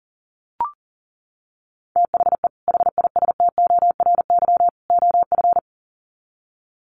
Morse code, heard as a keyed tone, about 37 wpm, 700 Hz. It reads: THE HISTORY OF
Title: Sets of 3 Words 40wpm